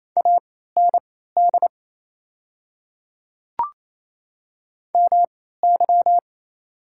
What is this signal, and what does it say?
Morse code, heard as a keyed tone, about 28 wpm, spaced as for 16 wpm, 700 Hz. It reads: AND MY
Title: 3x Character Spacing - Top 100 Words - 28wpm